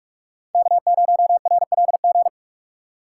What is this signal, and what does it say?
Morse code, heard as a keyed tone, about 41 wpm, 700 Hz. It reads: K0RLG